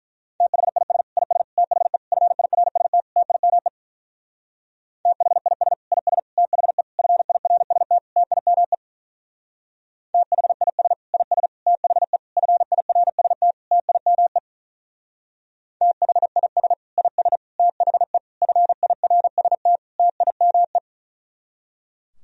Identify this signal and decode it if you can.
Morse code, heard as a keyed tone, about 45 wpm, 700 Hz. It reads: THIS IS THE FIRST TIME THIS IS THE FIRST TIME THIS IS THE FIRST TIME THIS IS THE FIRST TIME